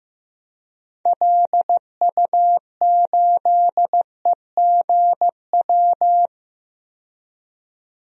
Morse code, heard as a keyed tone, about 15 wpm, 700 Hz. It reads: LU8EGW
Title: DXCC-100 Call Signs - 6 Characters - 15wpm